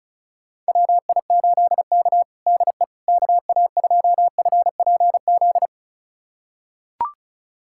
Morse code, heard as a keyed tone, about 36 wpm, 700 Hz. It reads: WI8K DE KA2FPZ
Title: USA Callsign DE Callsign 35wpm